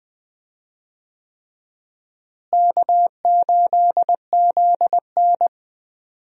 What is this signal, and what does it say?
Morse code, heard as a keyed tone, about 20 wpm, 700 Hz. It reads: K8ZN
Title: USA Call Signs - 4 Characters - Repeated 3x - 20wpm